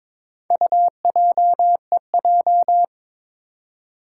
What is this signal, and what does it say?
Morse code, heard as a keyed tone, about 22 wpm, 700 Hz. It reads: UJEJ
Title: Four Challenging Letter-Number Combinations 22wpm